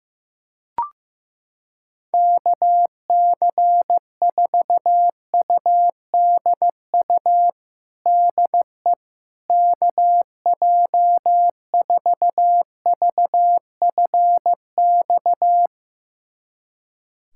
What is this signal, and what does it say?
Morse code, heard as a keyed tone, about 15 wpm, 700 Hz. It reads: KC4UDU DE KJ4VFX
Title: USA Callsign DE Callsign - 6 Characters - 15wpm